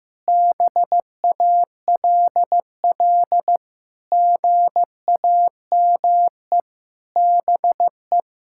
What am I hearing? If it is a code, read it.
Morse code, heard as a keyed tone, about 15 wpm, 700 Hz. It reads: BALL GAME BE
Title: Sentences from Top 1000 Words - Review Part 6 - 15wpm